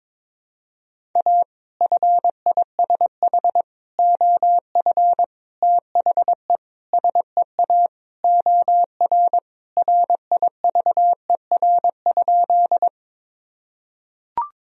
Morse code, heard as a keyed tone, about 22 wpm, 700 Hz. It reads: A FISH OF THE SEA OR RIVER?
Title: Sentences from Top 1000 Words - Review Part 1 - 22wpm